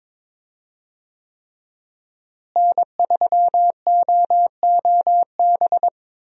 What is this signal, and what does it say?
Morse code, heard as a keyed tone, about 22 wpm, 700 Hz. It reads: N3OOB
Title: USA Call Signs - 5 Characters - Repeated 3x - 22wpm